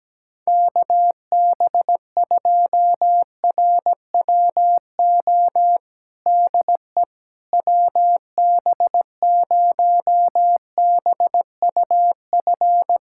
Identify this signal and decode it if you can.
Morse code, heard as a keyed tone, about 17 wpm, 700 Hz. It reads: KB2RWO DE WB0BUF
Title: USA Callsign DE Callsign - 6 Characters - 17wpm